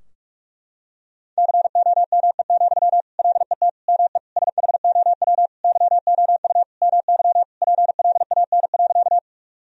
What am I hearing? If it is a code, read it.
Morse code, heard as a keyed tone, about 46 wpm, 700 Hz. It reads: COME, LET ME SHOW YOU MY PLAN.
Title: Sentences from Top 1000 Words - Review Part 2 - 45wpm